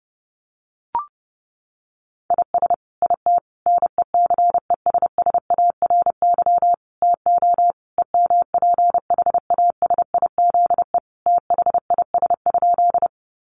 Morse code, heard as a keyed tone, about 30 wpm, 700 Hz. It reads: IS IT NECESSARY TO EMPHASIZE THIS?
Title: Sentences from Top 1500 Words 30wpm